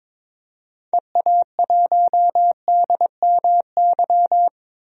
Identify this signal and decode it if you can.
Morse code, heard as a keyed tone, about 22 wpm, 700 Hz. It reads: EA1DMY
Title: DXCC-100 Call Signs - 6 Characters - 22wpm